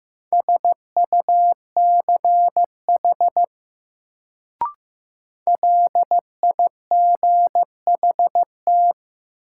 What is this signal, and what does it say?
Morse code, heard as a keyed tone, about 16 wpm, 700 Hz. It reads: SUCH LIGHT
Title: Sentences from Top 200 Words - New Word Drill - 15wpm